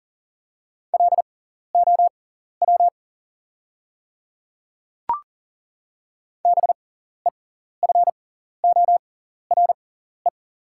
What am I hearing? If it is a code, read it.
Morse code, heard as a keyed tone, about 40 wpm, spaced as for 14 wpm, 700 Hz. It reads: LOW BEFORE
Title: 6x Character Spacing - Top 200 Words - 40wpm